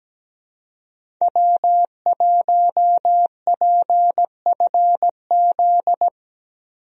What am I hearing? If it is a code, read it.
Morse code, heard as a keyed tone, about 17 wpm, 700 Hz. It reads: W1PFZ
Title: USA Call Signs - 5 Characters - Repeated 3x - 17wpm